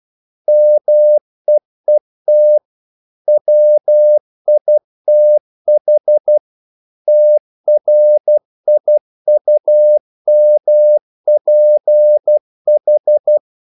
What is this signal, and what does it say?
Morse code, heard as a keyed tone, about 12 wpm, 600 Hz. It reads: MEET WITH TRIUMPH
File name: CW / Practice / If12